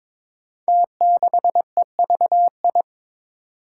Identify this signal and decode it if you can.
Morse code, heard as a keyed tone, about 22 wpm, 700 Hz. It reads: T6EVI
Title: Five Challenging Letter-Number Combinations 22wpm